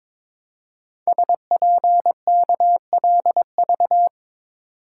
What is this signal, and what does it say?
Morse code, heard as a keyed tone, about 22 wpm, 700 Hz. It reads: SPKLV
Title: Code Groups - 5 Letters - 22wpm